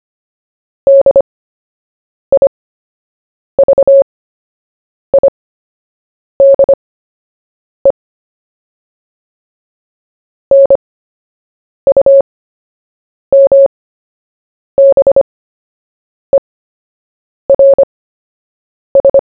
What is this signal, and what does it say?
Morse code, heard as a keyed tone, about 25 wpm, spaced as for 7 wpm, 550 Hz. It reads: DIVIDE NUMBERS
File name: cw / fists / Short-Story-1-7wpm